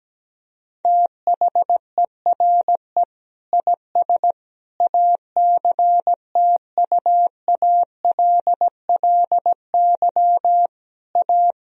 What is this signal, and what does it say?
Morse code, heard as a keyed tone, about 17 wpm, 700 Hz. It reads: THERE IS ACTUALLY A
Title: Sentences from Top 1000 Words - Review Part 4 - 17wpm